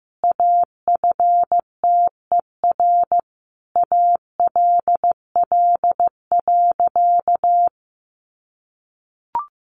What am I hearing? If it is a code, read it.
Morse code, heard as a keyed tone, about 15 wpm, 700 Hz. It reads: AFTER ALL.
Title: Sentences from Top 1300 Words 15wpm